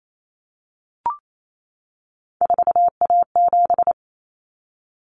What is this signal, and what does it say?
Morse code, heard as a keyed tone, about 28 wpm, 700 Hz. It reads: 4A7
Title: Common Cut Numbers - 3-Digit 28wpm